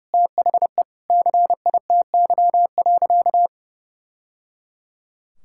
Morse code, heard as a keyed tone, about 30 wpm, 700 Hz. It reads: THE CITY.
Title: Sentences from Top 1000 Words - Review Part 6 - 30wpm